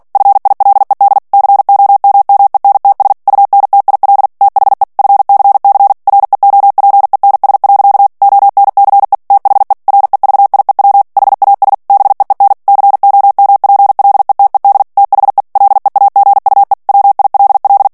Morse code, heard as a keyed tone, about 48 wpm, 800 Hz. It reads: FILED COMMENTS UNTIL THE FCC REOPENS. ONCE THE REVIEW HAS BEEN COMPLETED THE LEAGUE WILL
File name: ANGLAIS texte clair 3 vitesse 45 mots minute